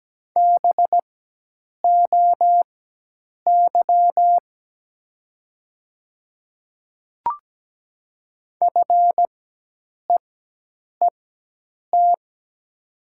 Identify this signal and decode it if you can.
Morse code, heard as a keyed tone, about 17 wpm, spaced as for 8 wpm, 700 Hz. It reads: BOY FEET
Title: 4x Character Spacing - Top 300 Words - 17wpm